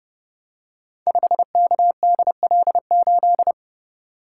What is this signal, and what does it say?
Morse code, heard as a keyed tone, about 29 wpm, 700 Hz. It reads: 5KDL8
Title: Code Groups - 5 Letter:Numbers - 30wpm